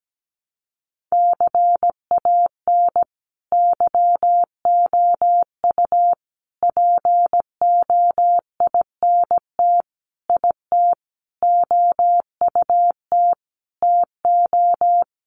Morse code, heard as a keyed tone, about 17 wpm, 700 Hz. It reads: CAN YOU POINT IT OUT TO
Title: Sentences from Top 200 Words 17wpm